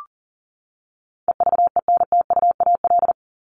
Morse code, heard as a keyed tone, about 40 wpm, 700 Hz. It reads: EVENTUAL
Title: LY Suffix - Top 100 Words with Base Words 40wpm